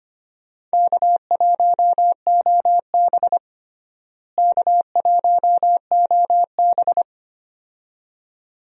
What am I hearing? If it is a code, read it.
Morse code, heard as a keyed tone, about 25 wpm, 700 Hz. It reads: K1OB K1OB
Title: USA Call Signs - 4 Characters - Repeated 2x - 25wpm